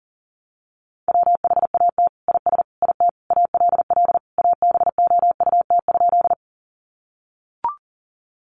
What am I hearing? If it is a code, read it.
Morse code, heard as a keyed tone, about 40 wpm, 700 Hz. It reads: WHAT IS IT ALL ABOUT?
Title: Sentences from Top 100 Words 40wpm